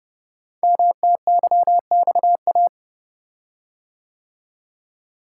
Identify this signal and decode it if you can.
Morse code, heard as a keyed tone, about 30 wpm, 700 Hz. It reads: MTYXA
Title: Code Groups - 5 Letters - 30wpm